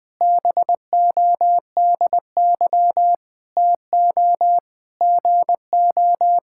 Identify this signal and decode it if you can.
Morse code, heard as a keyed tone, about 21 wpm, 700 Hz. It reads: BODY TO GO
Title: All-Age Appropriate Jokes - 20wpm